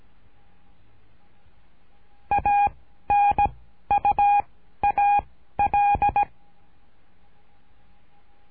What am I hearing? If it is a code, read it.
Morse code, heard as a keyed tone, about 17 wpm, 820 Hz. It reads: A N U A L
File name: cursos / CURSO DE TELEGRAFIA / Telegrafia - Trilha 13